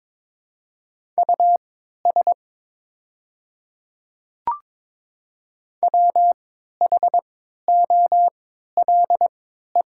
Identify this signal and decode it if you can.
Morse code, heard as a keyed tone, about 23 wpm, spaced as for 13 wpm, 690 Hz. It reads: US WHOLE
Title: 3x Character Spacing - Top 500 Words - 22wpm